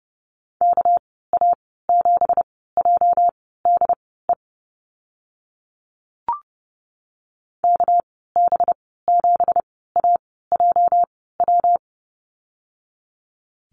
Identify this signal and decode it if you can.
Morse code, heard as a keyed tone, about 30 wpm, spaced as for 17 wpm, 700 Hz. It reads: KA7JDE KB7AJW
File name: Lesson 37g - Letter x - Call Signs 30wpm